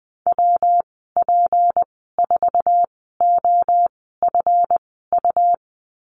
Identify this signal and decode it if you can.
Morse code, heard as a keyed tone, about 20 wpm, spaced as for 14 wpm, 700 Hz. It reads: WP4OFU
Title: 207 Lesson 33g - Letter j - Call Signs 20wpm